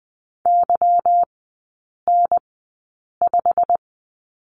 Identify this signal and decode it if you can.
Morse code, heard as a keyed tone, about 20 wpm, spaced as for 8 wpm, 690 Hz. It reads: YN5
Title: 145 Lesson 24e - Letter y - Three Characters 20wpm